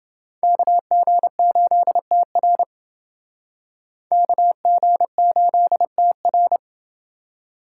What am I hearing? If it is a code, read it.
Morse code, heard as a keyed tone, about 28 wpm, 700 Hz. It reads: KG8TR KG8TR